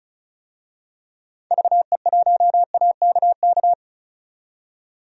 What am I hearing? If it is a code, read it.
Morse code, heard as a keyed tone, about 36 wpm, 700 Hz. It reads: VE1AKK
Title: DXCC-100 Call Signs - 6 Characters - 35wpm